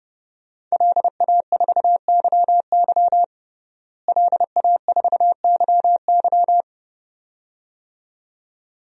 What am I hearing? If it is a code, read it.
Morse code, heard as a keyed tone, about 30 wpm, 700 Hz. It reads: LA4YY LA4YY
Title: DXCC-100 Call Signs - 5 Characters - Repeated 2x - 30wpm